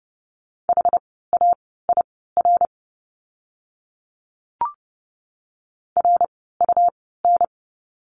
Morse code, heard as a keyed tone, about 30 wpm, spaced as for 17 wpm, 700 Hz. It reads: HAIR RUN
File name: Lesson 17f - Letter u - Words 30wpm